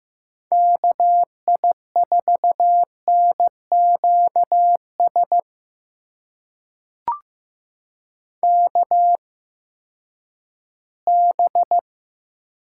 Morse code, heard as a keyed tone, about 15 wpm, spaced as for 10 wpm, 700 Hz. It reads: KI4NQS KB
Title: USA Call Signs - 6 Characters - 8x Character Spacing - 15wpm